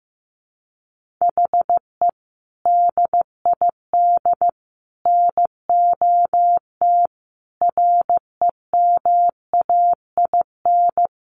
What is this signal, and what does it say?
Morse code, heard as a keyed tone, about 14 wpm, 700 Hz. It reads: HE DID NOT REMAIN
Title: Sentences from Top 1500 Words 15wpm